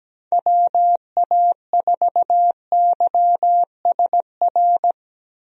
Morse code, heard as a keyed tone, about 17 wpm, 700 Hz. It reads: WA4YSR
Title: USA Call Signs - 6 Characters - Repeated 3x - 17wpm